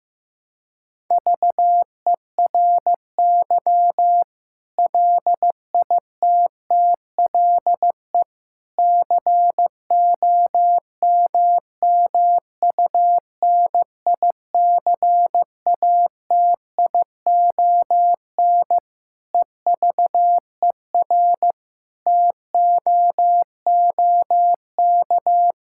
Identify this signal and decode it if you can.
Morse code, heard as a keyed tone, about 15 wpm, 700 Hz. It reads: VERY LITTLE COMMUNICATION EVER TOOK
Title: Sentences from Top 1000 Words - Review Part 6 - 15wpm